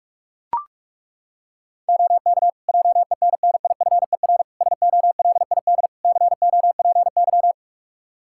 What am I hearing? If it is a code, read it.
Morse code, heard as a keyed tone, about 41 wpm, 700 Hz. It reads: OK JENNIFER SOLID COPY
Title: Practice QSO - Exchange 3 - More Info - 45wpm